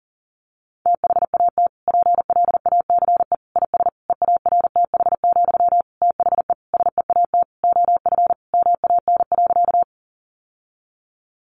Morse code, heard as a keyed tone, about 40 wpm, 700 Hz. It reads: THAT PLACE IS EARTH, THE SEAT OF MAN.
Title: Sentences from Top 1500 Words 40wpm